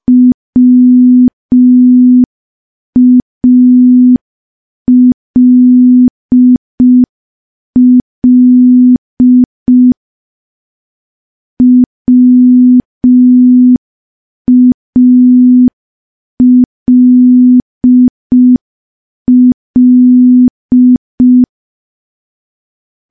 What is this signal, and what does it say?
Morse code, heard as a keyed tone, about 5 wpm, 250 Hz. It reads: WALL WALL
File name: morse / morse-training-05WPM